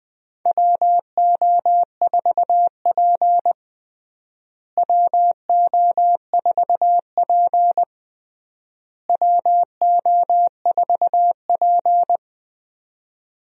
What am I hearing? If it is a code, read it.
Morse code, heard as a keyed tone, about 19 wpm, 700 Hz. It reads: WO4P WO4P WO4P
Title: USA Call Signs - 4 Characters - Repeated 3x - 20wpm